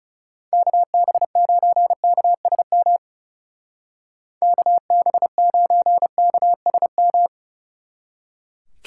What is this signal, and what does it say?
Morse code, heard as a keyed tone, about 36 wpm, 700 Hz. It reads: KB9KSM KB9KSM
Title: Speed Racing - USA 2x3 Call Signs 30wpm